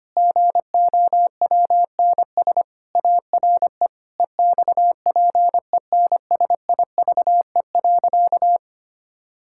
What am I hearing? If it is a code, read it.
Morse code, heard as a keyed tone, about 25 wpm, 700 Hz. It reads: GOWNS ARE EXPENSIVE.
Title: Pangram Challenge v2 - 25wpm